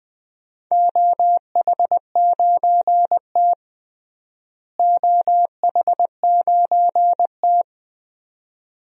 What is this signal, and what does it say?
Morse code, heard as a keyed tone, about 19 wpm, 700 Hz. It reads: OH9T OH9T
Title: DXCC-100 Call Signs - 4 Characters - Repeated 2x - 20wpm